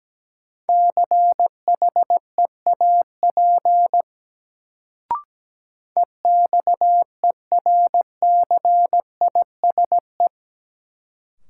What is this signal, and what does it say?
Morse code, heard as a keyed tone, about 17 wpm, 700 Hz. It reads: CHEAP EXERCISE
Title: Sentences from Top 1200 Words - New Word Drill Alt - 17wpm